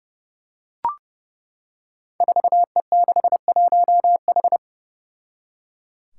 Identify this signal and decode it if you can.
Morse code, heard as a keyed tone, about 30 wpm, 700 Hz. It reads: 4E61H
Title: Five Challenging Letter-Number Combinations 30wpm